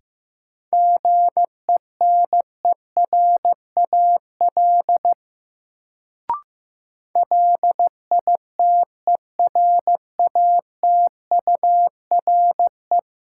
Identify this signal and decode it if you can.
Morse code, heard as a keyed tone, about 15 wpm, 700 Hz. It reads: GENERAL LITERATURE
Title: Sentences from Top 600 Words - New Word Drill - 15wpm